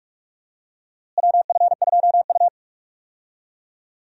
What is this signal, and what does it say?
Morse code, heard as a keyed tone, about 46 wpm, 700 Hz. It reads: WF2U
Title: USA Call Signs - 4 Characters - Repeated 3x - 45wpm